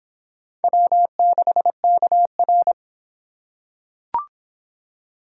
Morse code, heard as a keyed tone, about 26 wpm, 700 Hz. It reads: W6KR